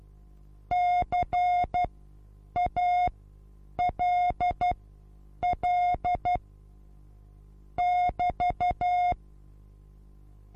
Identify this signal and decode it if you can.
Morse code, heard as a keyed tone, about 12 wpm, spaced as for 8 wpm, 720 Hz. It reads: CALL <BT>
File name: repos / Morsekurs / DJ4UF Morsen23